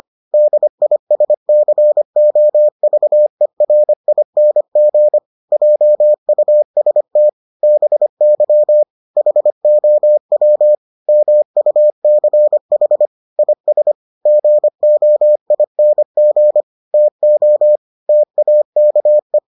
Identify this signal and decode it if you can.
Morse code, heard as a keyed tone, about 25 wpm, 600 Hz. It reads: DISCOVERING JUST BY HOW MUCH IS GOING TO TAKE